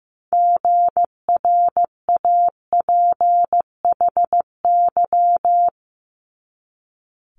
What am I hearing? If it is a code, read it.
Morse code, heard as a keyed tone, about 15 wpm, 700 Hz. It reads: GRAPHY
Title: Multi-letter Phonemes 15wpm